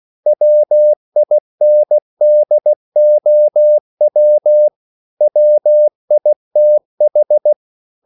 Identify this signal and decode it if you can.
Morse code, heard as a keyed tone, about 16 wpm, 600 Hz. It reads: WINDOW WITH